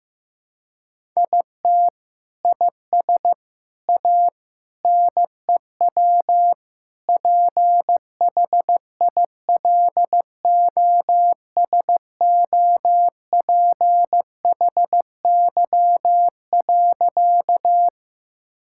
Morse code, heard as a keyed tone, about 15 wpm, 700 Hz. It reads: IT IS A NEW PHILOSOPHY.
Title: Sentences from Top 1000 Words - Review Part 6 - 15wpm